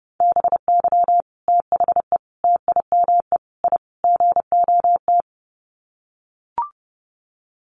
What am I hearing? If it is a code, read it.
Morse code, heard as a keyed tone, about 28 wpm, 700 Hz. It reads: BY THE TIME I GOT
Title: Sets of 5 Words Encore2 30wpm